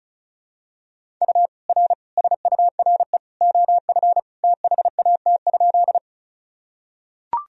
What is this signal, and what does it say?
Morse code, heard as a keyed tone, about 34 wpm, 700 Hz. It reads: U R SURE OF THAT?
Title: Sentences from Top 300 Words and Abbreviations 35wpm